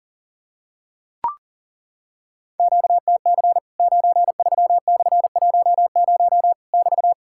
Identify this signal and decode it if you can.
Morse code, heard as a keyed tone, about 40 wpm, 700 Hz. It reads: QTC 93/10 <BT>
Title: Worked All Europe DX Contest - QTC Exchange - 40wpm